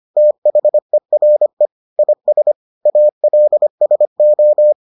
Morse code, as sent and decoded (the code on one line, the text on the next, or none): - .... .-. / .. ... / .- .-.. ... ---
THERE IS ALSO